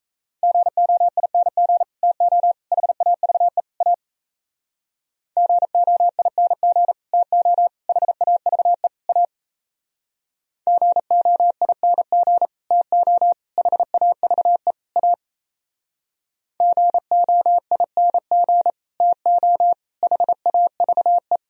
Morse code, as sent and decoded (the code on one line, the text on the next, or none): --. --- .. -. --. / - --- / .... .- ...- . / .- / --. --- .. -. --. / - --- / .... .- ...- . / .- / --. --- .. -. --. / - --- / .... .- ...- . / .- / --. --- .. -. --. / - --- / .... .- ...- .
GOING TO HAVE A GOING TO HAVE A GOING TO HAVE A GOING TO HAVE